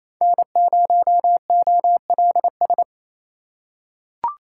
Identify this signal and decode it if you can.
Morse code, heard as a keyed tone, about 27 wpm, 700 Hz. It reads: N0OLS
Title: USA Call Signs - 4x Character Spacing - 28wpm